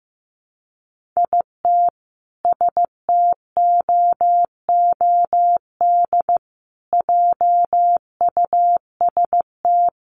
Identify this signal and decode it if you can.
Morse code, heard as a keyed tone, about 15 wpm, 700 Hz. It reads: IT STOOD JUST
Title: Sentences from Top 1400 Words 15wpm